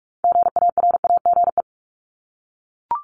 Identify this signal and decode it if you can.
Morse code, heard as a keyed tone, about 42 wpm, 700 Hz. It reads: GARAGE
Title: Multi-letter Phonemes Repeated 3x 45wpm